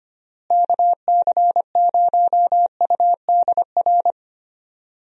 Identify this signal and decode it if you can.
Morse code, heard as a keyed tone, about 25 wpm, 700 Hz. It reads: KC0UDR
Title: USA Call Signs - 6 Characters - Repeated 2x - 25wpm